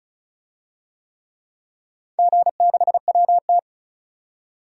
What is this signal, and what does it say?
Morse code, heard as a keyed tone, about 34 wpm, 700 Hz. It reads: G6WT
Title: DXCC-100 Call Signs - 4 Characters - 35wpm